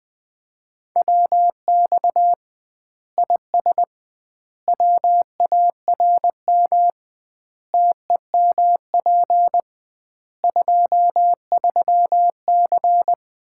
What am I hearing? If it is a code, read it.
Morse code, heard as a keyed tone, about 21 wpm, 700 Hz. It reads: WX IS WARM TEMP 23C
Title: Common QSO Phrases - 2x Word-Spacing - 20wpm